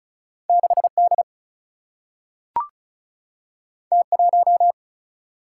Morse code, heard as a keyed tone, about 36 wpm, 700 Hz. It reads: 6D T1
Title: Single-Letter Single-Number 35wpm